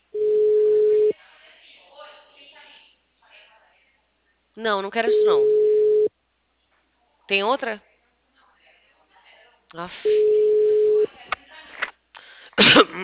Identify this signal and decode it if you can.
Ringback tone (Europe, 425 Hz), with speech.